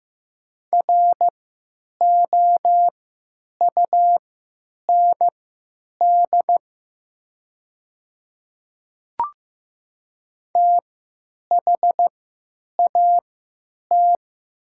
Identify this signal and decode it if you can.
Morse code, heard as a keyed tone, about 15 wpm, spaced as for 9 wpm, 700 Hz. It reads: ROUND THAT
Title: 3x Character Spacing - Top 200 Words - 15wpm